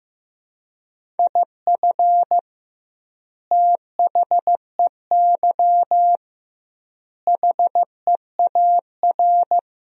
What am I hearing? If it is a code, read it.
Morse code, heard as a keyed tone, about 15 wpm, 700 Hz. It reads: IF THEY HEAR
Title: 2x Word-Spacing - Sentences from Top 400 Words - 15wpm